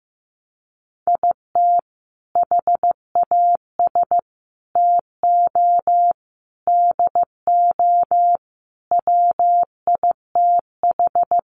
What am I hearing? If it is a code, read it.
Morse code, heard as a keyed tone, about 15 wpm, 700 Hz. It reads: IT HAS TO DO WITH
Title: Sets of 5 Words 15wpm